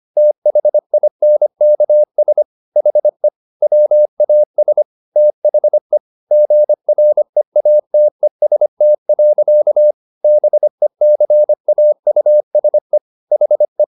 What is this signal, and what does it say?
Morse code, heard as a keyed tone, about 25 wpm, 600 Hz. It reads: THINKS HE WAS THE GREATEST. BECAUSE HE
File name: the old man and the sea part 1 25wpm